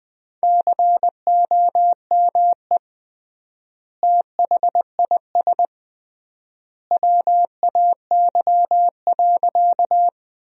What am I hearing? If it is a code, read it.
Morse code, heard as a keyed tone, about 20 wpm, 700 Hz. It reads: COME THIS WAY.
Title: 3x Word-Spacing - Sentences from Top 100 Words - 20wpm